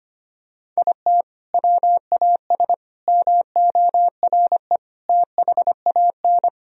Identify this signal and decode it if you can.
Morse code, heard as a keyed tone, about 25 wpm, 700 Hz. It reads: IT WAS MORE THAN
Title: Sentences from Top 1000 Words - Review Part 3 - 25wpm